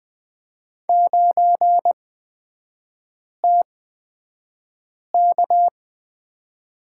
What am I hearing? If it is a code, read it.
Morse code, heard as a keyed tone, about 20 wpm, spaced as for 5 wpm, 700 Hz. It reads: 9TK
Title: 224 Lesson 36e - BT - Three Characters 20wpm